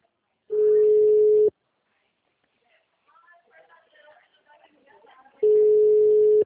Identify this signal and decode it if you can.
Ringback tone (Europe, 425 Hz).